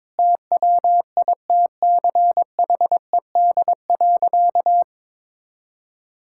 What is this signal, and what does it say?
Morse code, heard as a keyed tone, about 22 wpm, 700 Hz. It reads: TWITCHED.
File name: practice / 22wpm / Sentences with 2-Word Phrases - 22wpm